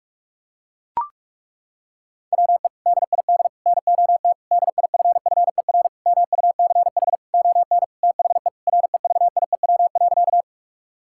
Morse code, heard as a keyed tone, about 41 wpm, 700 Hz. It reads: WE DID NOT DIFFER MUCH ON THE REVIEW.
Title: Sentences from Top 1000 Words - Review Part 6 - 45wpm